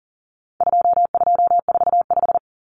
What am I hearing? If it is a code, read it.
Morse code, heard as a keyed tone, about 40 wpm, 700 Hz. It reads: 2245
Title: Numbers - 4-Digit 40wpm